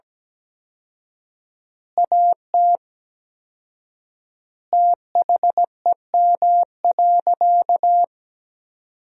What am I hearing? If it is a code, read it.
Morse code, heard as a keyed tone, about 17 wpm, 700 Hz. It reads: AT THEM.
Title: 4x Word-Spacing - Sentences from Top 100 Words - 17wpm